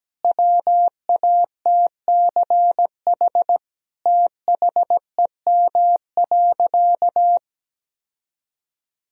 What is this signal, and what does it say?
Morse code, heard as a keyed tone, about 17 wpm, 700 Hz. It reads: WATCH THEM.
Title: Sentences from Top 1000 Words - Review Part 4 - 17wpm